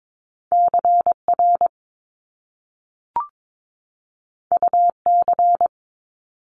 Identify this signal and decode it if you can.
Morse code, heard as a keyed tone, about 22 wpm, 700 Hz. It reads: CR UC
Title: Top Two Letter Combinations in Words 22wpm